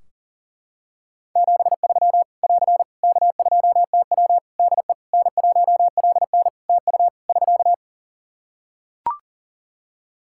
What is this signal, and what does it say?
Morse code, heard as a keyed tone, about 41 wpm, 700 Hz. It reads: 73 <AR> K2TW DE N1LN TU <SK>
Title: Practice QSO - Exchange 4 - Ending - 40wpm